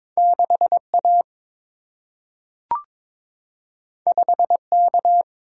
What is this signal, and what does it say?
Morse code, heard as a keyed tone, about 22 wpm, 700 Hz. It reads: 6 A 5 K
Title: Single-Letter Single-Number 22wpm